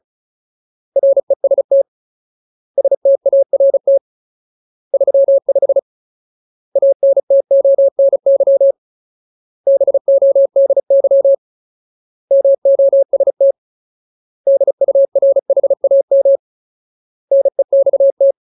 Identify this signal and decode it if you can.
Morse code, heard as a keyed tone, about 34 wpm, 550 Hz. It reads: REST START 35 ANTONY BODY MOST DURHAM NEXT
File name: Ultimate Word List / 35WPM Ultimate